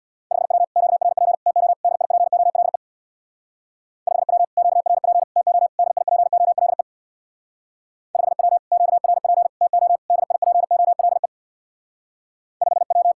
Morse code, heard as a keyed tone, about 65 wpm, 700 Hz. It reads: 5W QRP TO DIPOLE 5W QRP TO DIPOLE 5W QRP TO DIPOLE 5W